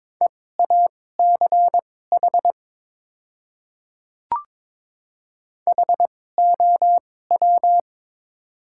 Morse code, heard as a keyed tone, about 22 wpm, spaced as for 16 wpm, 700 Hz. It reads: EACH HOW
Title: 2x Character Spacing - Top 100 Words - 22wpm